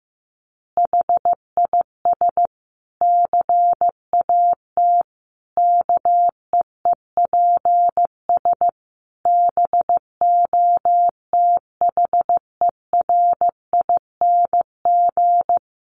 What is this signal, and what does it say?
Morse code, heard as a keyed tone, about 15 wpm, 700 Hz. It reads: HIS CAT KEEPS BOTHERING
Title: Additional Easy Phrases 15wpm